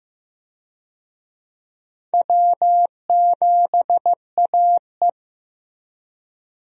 Morse code, heard as a keyed tone, about 15 wpm, 700 Hz. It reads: W7AE
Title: USA Call Signs - 4 Characters - Repeated 2x - 15wpm